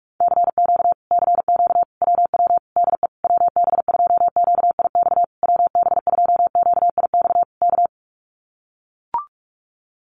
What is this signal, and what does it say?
Morse code, heard as a keyed tone, about 44 wpm, 700 Hz. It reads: CQ CQ WW DE WB2QIX WB2QIX K
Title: Common QSO Phrases Encore 45wpm